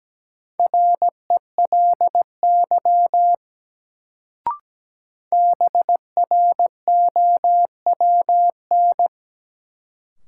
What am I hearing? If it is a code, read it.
Morse code, heard as a keyed tone, about 17 wpm, 700 Hz. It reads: RELY BROWN
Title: Sentences from Top 1200 Words - New Word Drill - 17wpm